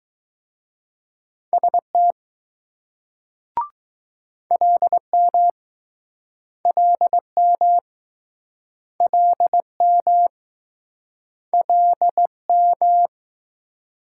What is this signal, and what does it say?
Morse code, heard as a keyed tone, about 20 wpm, 700 Hz. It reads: ST LM LM LM LM